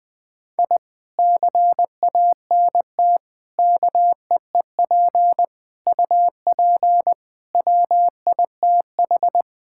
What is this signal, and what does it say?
Morse code, heard as a keyed tone, about 21 wpm, 700 Hz. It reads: I CANT KEEP UP WITH
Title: Sentences with 2-Word Phrases - Spoken Beginning - 20wpm